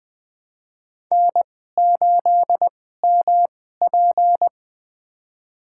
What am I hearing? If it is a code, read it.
Morse code, heard as a keyed tone, about 20 wpm, spaced as for 14 wpm, 700 Hz. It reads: N8MP
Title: USA Call Signs - 4 Characters - 2x Character Spacing - 20wpm